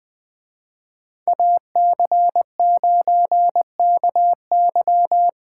Morse code, heard as a keyed tone, about 20 wpm, 700 Hz. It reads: AC9KY